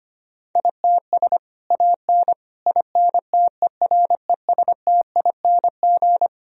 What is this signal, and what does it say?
Morse code, heard as a keyed tone, about 25 wpm, 700 Hz. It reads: ITS AN INTERESTING
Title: Sentences from Top 1000 Words - Review Part 4 - 25wpm